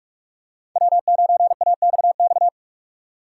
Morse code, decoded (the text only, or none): W9AXX